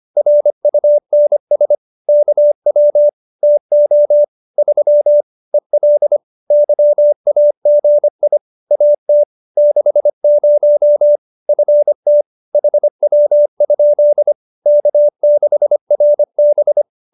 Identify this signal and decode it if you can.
Morse code, heard as a keyed tone, about 25 wpm, 590 Hz. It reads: RUNS KW TO 3 EL YAGI AT 60 FT HW? K6RB